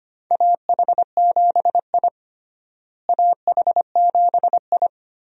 Morse code, heard as a keyed tone, about 25 wpm, 700 Hz. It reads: AH7I AH7I
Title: USA Call Signs - 4 Characters - Repeated 3x - 25wpm